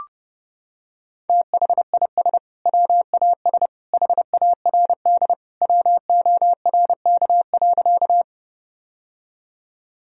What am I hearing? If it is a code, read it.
Morse code, heard as a keyed tone, about 30 wpm, 700 Hz. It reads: THIS WAS HARD WORK.
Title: Sentences from Top 1000 Words - Review Part 1 - 30wpm